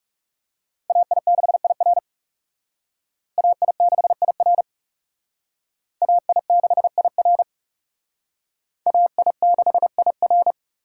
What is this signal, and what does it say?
Morse code, heard as a keyed tone, about 37 wpm, 700 Hz. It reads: AI6IR AI6IR AI6IR AI6IR
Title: Speed Racing - USA 2x2 Call Signs 30wpm